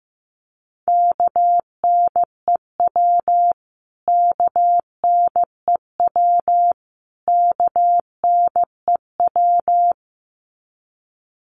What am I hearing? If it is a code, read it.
Morse code, heard as a keyed tone, about 15 wpm, 700 Hz. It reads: KNEW KNEW KNEW
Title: Top 500 Words Repeated 15wpm